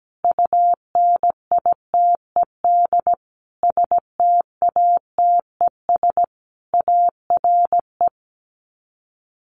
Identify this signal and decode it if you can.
Morse code, heard as a keyed tone, about 17 wpm, 700 Hz. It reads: UNITED STATES ARE
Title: Sets of 5 Words Encore 17wpm